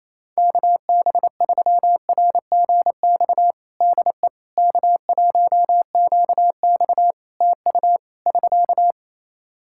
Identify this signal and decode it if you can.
Morse code, heard as a keyed tone, about 28 wpm, 700 Hz. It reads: KB3RGX DE K1QX TU <SK>
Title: Practice QSO - Exchange 4 - Ending - 28wpm